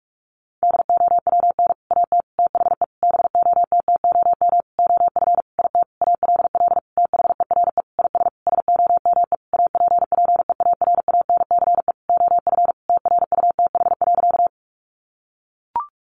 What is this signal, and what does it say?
Morse code, heard as a keyed tone, about 44 wpm, 700 Hz. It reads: DOWN AT THE BOTTOM OF IT ALL THERE IS SOME APPEARANCE OF TRUTH.
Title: Sentences from Top 1100 Words 45wpm